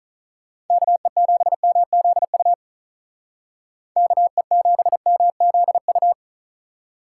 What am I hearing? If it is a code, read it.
Morse code, heard as a keyed tone, about 43 wpm, 700 Hz. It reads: KE7MZU KE7MZU
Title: Speed Racing - USA 2x3 Call Signs 35wpm